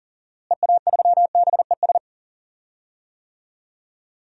Morse code, heard as a keyed tone, about 42 wpm, 700 Hz. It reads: EA3BES
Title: DXCC-100 Call Signs - 6 Characters - Repeated 3x - 40wpm